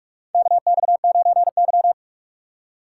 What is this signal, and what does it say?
Morse code, heard as a keyed tone, about 43 wpm, 700 Hz. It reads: KX9Y
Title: USA Call Signs - 4 Characters - 6x Character Spacing - 45wpm